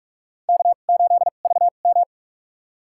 Morse code, heard as a keyed tone, about 47 wpm, spaced as for 33 wpm, 700 Hz. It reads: K8VM
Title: USA Call Signs - 4 Characters - 2x Character Spacing - 45wpm